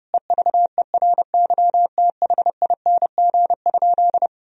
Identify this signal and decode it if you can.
Morse code, heard as a keyed tone, about 30 wpm, 700 Hz. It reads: EVERYTHING?
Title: Sentences from Top 1000 Words - Review Part 2 - 30wpm